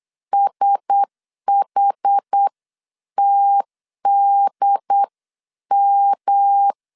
Morse code, heard as a keyed tone, about 8 wpm, 800 Hz. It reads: SHTDM
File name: 8wpm